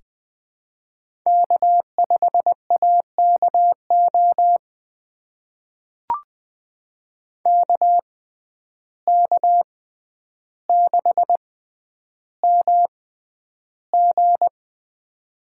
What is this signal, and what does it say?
Morse code, heard as a keyed tone, about 20 wpm, spaced as for 10 wpm, 700 Hz. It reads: K5AKO KK6MG
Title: USA Call Signs - 5 Characters - 6x Character Spacing - 20wpm